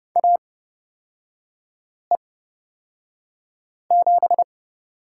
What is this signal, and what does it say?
Morse code, heard as a keyed tone, about 30 wpm, spaced as for 5 wpm, 700 Hz. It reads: AE7
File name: Lesson 37e - Letter x - Three Characters 30wpm